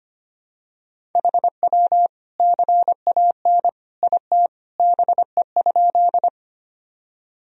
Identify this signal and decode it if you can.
Morse code, heard as a keyed tone, about 25 wpm, 700 Hz. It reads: HW CAN IT BE?
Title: Sentences from Top 100 Words and Abbreviations 25wpm